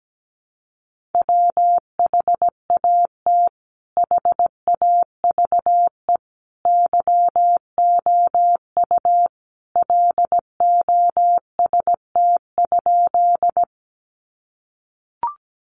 Morse code, heard as a keyed tone, about 17 wpm, 700 Hz. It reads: WHAT HAVE YOU LOST?
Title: Sentences from Top 1400 Words 17wpm